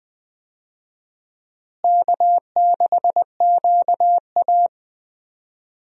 Morse code, heard as a keyed tone, about 20 wpm, 700 Hz. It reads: K6QA
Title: USA Call Signs - 4 Characters - Repeated 2x - 20wpm